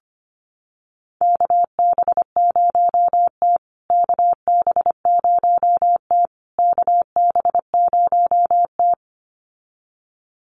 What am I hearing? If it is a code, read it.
Morse code, heard as a keyed tone, about 25 wpm, 700 Hz. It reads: KB0T KB0T KB0T
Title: USA Call Signs Repeated 3x 25wpm